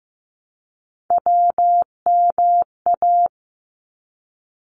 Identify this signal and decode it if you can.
Morse code, heard as a keyed tone, about 15 wpm, 700 Hz. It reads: WMA
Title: ARRL Sections 15wpm